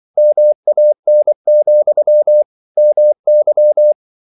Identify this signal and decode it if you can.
Morse code, heard as a keyed tone, about 24 wpm, 600 Hz. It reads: MAN, MY
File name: CW / Practice / If24